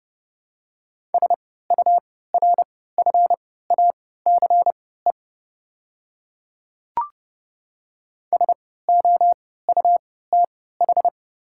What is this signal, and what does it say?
Morse code, heard as a keyed tone, about 29 wpm, spaced as for 17 wpm, 700 Hz. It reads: SURFACE SOUTH
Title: 3x Character Spacing - Top 500 Words - 30wpm